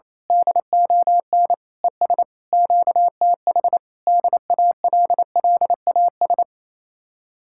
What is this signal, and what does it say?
Morse code, heard as a keyed tone, about 27 wpm, 700 Hz. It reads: DON ES QTH DALLAS
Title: Speed-Racing Common QSO Phrases Encore - 28wpm